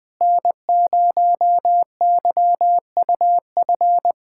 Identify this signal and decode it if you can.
Morse code, heard as a keyed tone, about 21 wpm, 700 Hz. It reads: N0YUF
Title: USA Callsign DE Callsign - 5 Characters - 20wpm